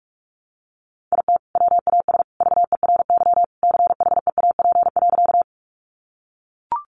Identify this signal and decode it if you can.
Morse code, heard as a keyed tone, about 44 wpm, 700 Hz. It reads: IT WAS VERY CHEAP.
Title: Sentences from Top 1200 Words 45wpm